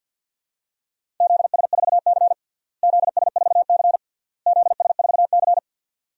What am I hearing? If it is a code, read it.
Morse code, heard as a keyed tone, about 51 wpm, 700 Hz. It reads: ZS4C ZS4C ZS4C
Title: DXCC-100 Call Signs - 4 Characters - Repeated 3x - 50wpm